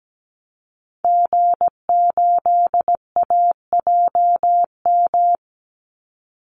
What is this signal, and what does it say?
Morse code, heard as a keyed tone, about 17 wpm, 700 Hz. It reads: G8AJM
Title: Call Signs Repeated 2x - 17wpm